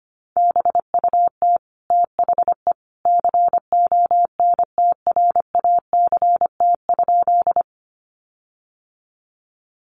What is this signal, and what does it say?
Morse code, heard as a keyed tone, about 25 wpm, 700 Hz. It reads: BUT THE CONTRACT?
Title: Sentences from Top 1400 Words 25wpm